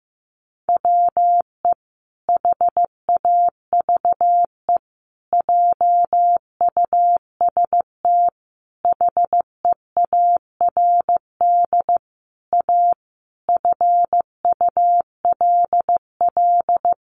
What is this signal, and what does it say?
Morse code, heard as a keyed tone, about 15 wpm, 700 Hz. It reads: WE HAVE JUST HEARD A FULL
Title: Sentences from Top 900 Words 15wpm